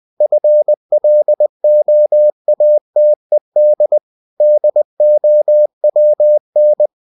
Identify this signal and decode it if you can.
Morse code, heard as a keyed tone, about 20 wpm, 600 Hz. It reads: FLOATED DOWN